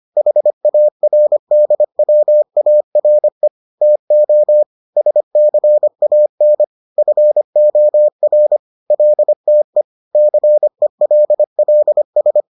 Morse code, heard as a keyed tone, about 25 wpm, 600 Hz. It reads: HARDWARE TO SCAN FOR LTE CELLS